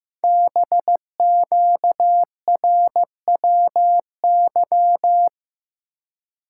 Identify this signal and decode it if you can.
Morse code, heard as a keyed tone, about 15 wpm, 700 Hz. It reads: BQRWY
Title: Code Groups - 5 Letters - 15wpm